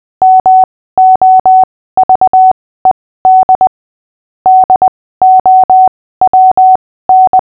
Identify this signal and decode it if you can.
Morse code, heard as a keyed tone, about 20 wpm, spaced as for 15 wpm, 750 Hz. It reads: MOVED DOWN
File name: cw / fists / Short-Story-5-15wpm